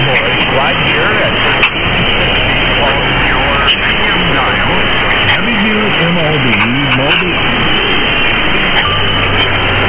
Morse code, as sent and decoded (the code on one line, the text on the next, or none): none